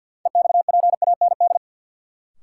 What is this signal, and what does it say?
Morse code, heard as a keyed tone, about 48 wpm, 700 Hz. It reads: EXPAND
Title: Sentences from Top 1400 Words - New Word Drill Alt - 50wpm